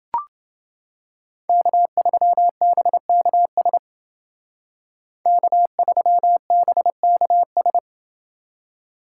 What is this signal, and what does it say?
Morse code, heard as a keyed tone, about 28 wpm, 700 Hz. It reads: K3BKS K3BKS